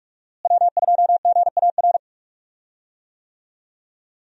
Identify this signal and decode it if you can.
Morse code, heard as a keyed tone, about 45 wpm, 700 Hz. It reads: W2GAR
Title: USA Call Signs - 5 Characters - Repeated 2x - 45wpm